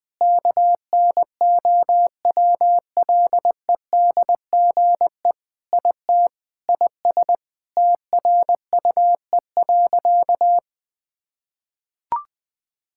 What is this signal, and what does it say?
Morse code, heard as a keyed tone, about 20 wpm, 700 Hz. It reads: KNOWLEDGE IT IS TRUE.
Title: Sentences with 6-Word Phrases - Spoken Beginning - 20wpm